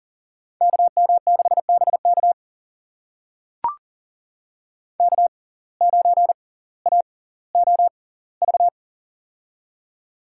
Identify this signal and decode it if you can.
Morse code, heard as a keyed tone, about 40 wpm, spaced as for 20 wpm, 700 Hz. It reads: KM6BK K9AOV
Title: USA Call Signs - 5 Characters - 6x Character Spacing - 40wpm